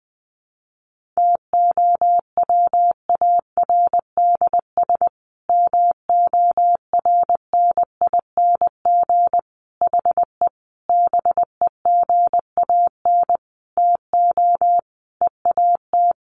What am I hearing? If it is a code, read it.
Morse code, heard as a keyed tone, about 20 wpm, 700 Hz. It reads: TOWARDS MORNING HE BEGAN TO EAT
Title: Sentences from Top 1100 Words 20wpm